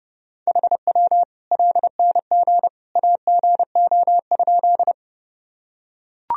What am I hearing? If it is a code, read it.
Morse code, heard as a keyed tone, about 30 wpm, 700 Hz. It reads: HW LNG AGO?
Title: Sentences from Top 500 Words and Abbreviations 30wpm